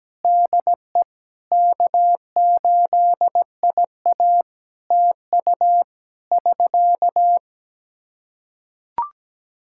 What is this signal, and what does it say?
Morse code, heard as a keyed tone, about 17 wpm, 700 Hz. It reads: DE K8IA TU <SK>
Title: Practice QSO - Exchange 4 - Ending - 17wpm